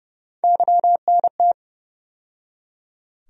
Morse code, heard as a keyed tone, about 30 wpm, 700 Hz. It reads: YNT
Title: USA Call Sign Postfixes - 3 Letters 30wpm